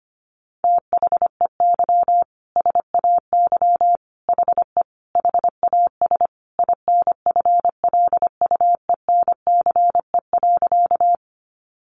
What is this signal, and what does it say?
Morse code, heard as a keyed tone, about 25 wpm, 700 Hz. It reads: THEY SAY HE HAS INFLUENCE.
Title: Sentences from Top 1100 Words 25wpm